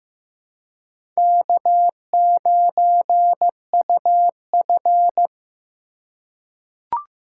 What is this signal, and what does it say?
Morse code, heard as a keyed tone, about 15 wpm, 700 Hz. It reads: K9UF
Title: USA Call Signs - 4x Character Spacing - 15wpm